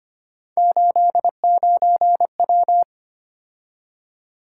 Morse code, heard as a keyed tone, about 25 wpm, 700 Hz. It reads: 89W
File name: Three Letter-Number Combinations - 25wpm